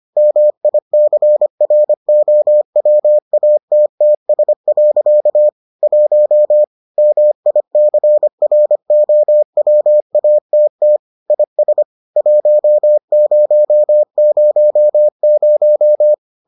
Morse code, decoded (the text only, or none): MICROWATTS. 1 MICROWATT IS 1000